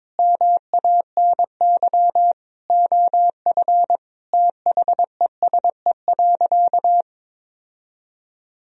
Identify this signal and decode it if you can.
Morse code, heard as a keyed tone, about 22 wpm, 700 Hz. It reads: MANY OF THESE.
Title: Sentences from Top 1000 Words - Review Part 1 - 22wpm